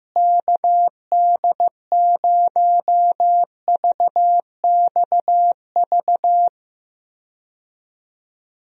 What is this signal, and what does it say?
Morse code, heard as a keyed tone, about 15 wpm, 700 Hz. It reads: KD0VXV
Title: USA Call Signs - 6 Characters - Repeated 3x - 15wpm